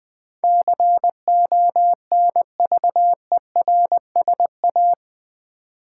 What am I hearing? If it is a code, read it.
Morse code, heard as a keyed tone, about 19 wpm, 700 Hz. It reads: CONVERSA
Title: Build A Word - 4 Letters At A Time - 12-Letter Words - 20wpm